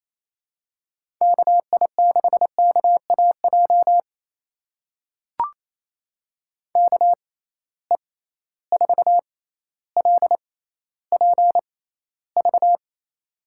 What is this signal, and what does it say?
Morse code, heard as a keyed tone, about 28 wpm, spaced as for 14 wpm, 700 Hz. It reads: KI6KAJ KE4LPV
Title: USA Call Signs - 6 Characters - 6x Character Spacing - 28wpm